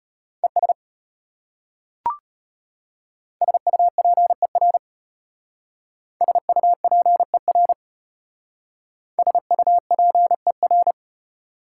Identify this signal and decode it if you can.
Morse code, heard as a keyed tone, about 32 wpm, 690 Hz. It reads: ES SUPER SUPER SUPER